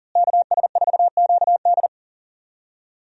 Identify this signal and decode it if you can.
Morse code, heard as a keyed tone, about 38 wpm, 700 Hz. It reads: KS4QD
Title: USA Call Signs - 5 Characters - Repeated 2x - 40wpm